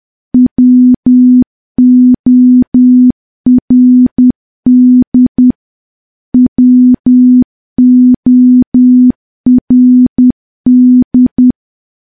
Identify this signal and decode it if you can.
Morse code, heard as a keyed tone, about 10 wpm, 250 Hz. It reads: WORD WORD
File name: morse / morse-training-10WPM